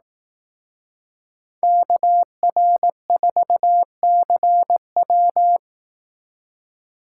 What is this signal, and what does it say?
Morse code, heard as a keyed tone, about 18 wpm, 700 Hz. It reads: KR4CW
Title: Speed Racing - USA Call Signs 15wpm